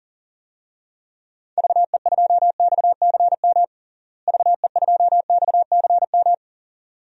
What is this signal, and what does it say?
Morse code, heard as a keyed tone, about 40 wpm, 700 Hz. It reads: VE2XCM VE2XCM
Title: DXCC-100 Call Signs - 6 Characters - Repeated 2x - 40wpm